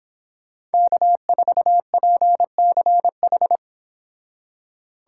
Morse code, heard as a keyed tone, about 30 wpm, 700 Hz. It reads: K4PCH